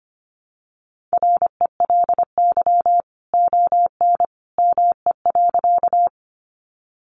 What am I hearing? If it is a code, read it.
Morse code, heard as a keyed tone, about 25 wpm, 700 Hz. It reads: RELY ON ME.
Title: Sentences from Top 1200 Words 25wpm